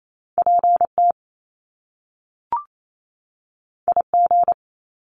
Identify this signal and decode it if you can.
Morse code, heard as a keyed tone, about 28 wpm, 700 Hz. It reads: PT IG
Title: Top Two Letter Combinations in Words 28wpm